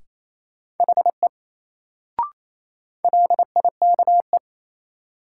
Morse code, heard as a keyed tone, about 28 wpm, 700 Hz. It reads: HE LIKE
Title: Sentences from Top 100 Words - New Word Drill - 28wpm